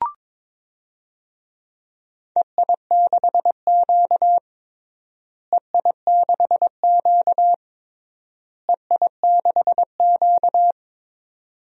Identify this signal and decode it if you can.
Morse code, heard as a keyed tone, about 22 wpm, 700 Hz. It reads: EI6Q EI6Q EI6Q
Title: DXCC-100 Call Signs - 4 Characters - Repeated 3x - 22wpm